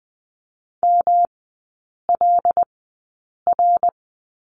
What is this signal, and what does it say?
Morse code, heard as a keyed tone, about 20 wpm, spaced as for 8 wpm, 700 Hz. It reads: MLR
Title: 121 Lesson 20e - Number 3 - Three Characters 20wpm